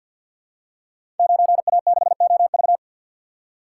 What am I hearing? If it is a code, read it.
Morse code, heard as a keyed tone, about 50 wpm, 700 Hz. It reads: 9A6OV
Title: Code Groups - 5 Letter:Numbers - 50wpm